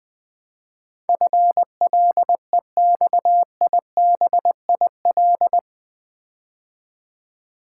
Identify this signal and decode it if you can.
Morse code, heard as a keyed tone, about 20 wpm, 700 Hz. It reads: FLEXIBIL
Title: Build A Word - 4 Letters At A Time - 11-Letter Words - 20wpm